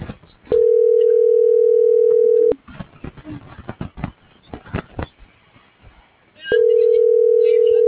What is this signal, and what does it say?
Ringback tone (North America), 440 + 480 Hz.